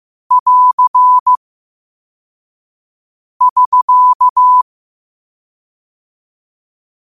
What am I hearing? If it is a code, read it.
Morse code, heard as a keyed tone, about 15 wpm, 1000 Hz. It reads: <AR> <SK>